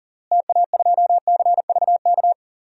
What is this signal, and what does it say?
Morse code, heard as a keyed tone, about 37 wpm, 700 Hz. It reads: TA2CVK
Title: DXCC-100 Call Signs - 6 Characters - Repeated 3x - 40wpm